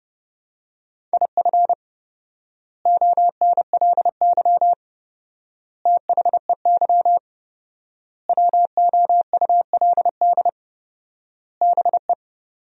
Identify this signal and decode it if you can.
Morse code, heard as a keyed tone, about 30 wpm, 700 Hz. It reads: IF ONLY THEY WOULD BE
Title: 4x Word-Spacing - Sentences from Top 500 Words - 30wpm